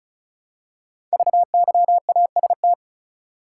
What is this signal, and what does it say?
Morse code, heard as a keyed tone, about 36 wpm, 700 Hz. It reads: VYAST
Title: Code Groups - 5 Letters - 35wpm